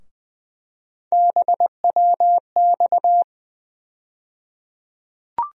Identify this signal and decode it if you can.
Morse code, heard as a keyed tone, about 21 wpm, 700 Hz. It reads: BWX
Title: USA Call Sign Postfixes - 3 Letters 20wpm